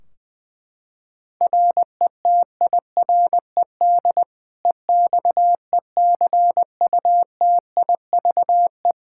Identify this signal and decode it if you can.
Morse code, heard as a keyed tone, about 20 wpm, 700 Hz. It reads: RETIRED EXECUTIVE
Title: K1USN SST QSO Practice 20wpm